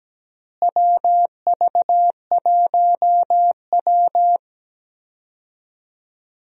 Morse code, heard as a keyed tone, about 17 wpm, 700 Hz. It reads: WV1W
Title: USA Call Signs - 4 Characters - Repeated 3x - 17wpm